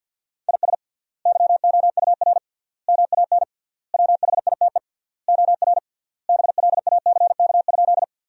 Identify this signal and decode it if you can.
Morse code, heard as a keyed tone, about 52 wpm, 700 Hz. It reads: IS YOUR MAN WHITE OR BLACK?
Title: 3x Word-Spacing - Sentences from Top 400 Words - 50wpm